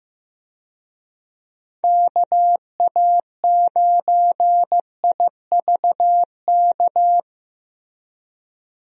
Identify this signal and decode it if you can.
Morse code, heard as a keyed tone, about 15 wpm, 700 Hz. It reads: KA9IVK
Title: USA Call Signs - 6 Characters - Repeated 2x - 15wpm